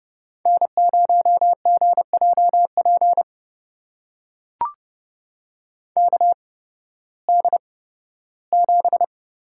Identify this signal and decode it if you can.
Morse code, heard as a keyed tone, about 32 wpm, spaced as for 16 wpm, 700 Hz. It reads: N0GJP KD7
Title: USA Call Signs - 8x Character Spacing - 30wpm